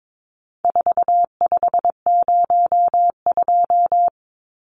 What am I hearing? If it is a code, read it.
Morse code, heard as a keyed tone, about 22 wpm, 700 Hz. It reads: 4502
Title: Numbers - 4-Digit 22wpm